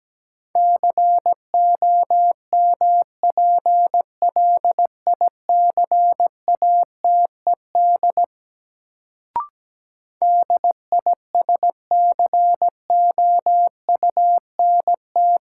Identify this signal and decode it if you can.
Morse code, heard as a keyed tone, about 17 wpm, 700 Hz. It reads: COMPLICATED DISCOUNT
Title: Sentences from Top 1200 Words - New Word Drill Alt - 17wpm